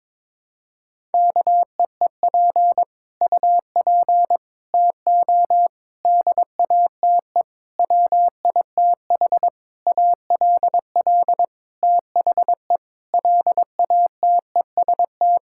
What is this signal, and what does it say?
Morse code, heard as a keyed tone, about 22 wpm, 700 Hz. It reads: KEEP UP TO DATE WITH ALL THE LATEST
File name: Sets of 9 Words - 22wpm